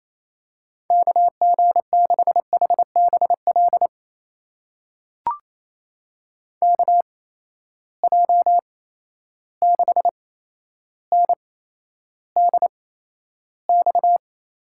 Morse code, heard as a keyed tone, about 28 wpm, spaced as for 12 wpm, 700 Hz. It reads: KG6HBL KJ6NDX
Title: USA Call Signs - 6 Characters - 8x Character Spacing - 28wpm